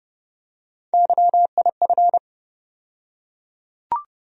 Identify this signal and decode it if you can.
Morse code, heard as a keyed tone, about 30 wpm, 700 Hz. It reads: YIF
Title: USA Call Sign Postfixes - 3 Letters 30wpm